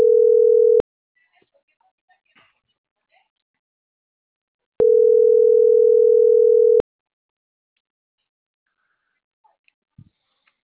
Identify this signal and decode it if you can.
Ringback tone (North America), 440 + 480 Hz.